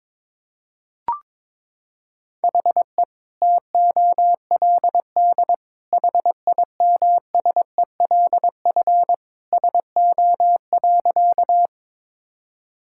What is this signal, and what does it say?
Morse code, heard as a keyed tone, about 22 wpm, 700 Hz. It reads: HE TOLD HIMSELF SO.
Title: Sentences from Top 1600 Words 22wpm